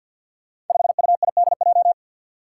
Morse code, heard as a keyed tone, about 51 wpm, 700 Hz. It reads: 5UIDJ
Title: Five Challenging Letter-Number Combinations 50wpm